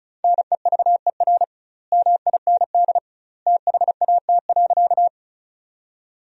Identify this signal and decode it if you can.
Morse code, heard as a keyed tone, about 34 wpm, 700 Hz. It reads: NEVER MIND THAT.
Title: 2x Word-Spacing - Sentences from Top 500 Words - 35wpm